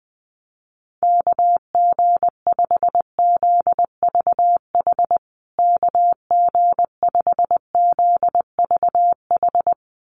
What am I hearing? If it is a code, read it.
Morse code, heard as a keyed tone, about 20 wpm, 700 Hz. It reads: KG5ZVH KG5ZVH
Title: USA Call Signs Repeated 2x 20wpm